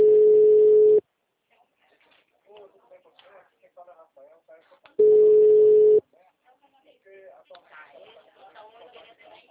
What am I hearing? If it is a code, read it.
Ringback tone (Europe, 425 Hz), with speech.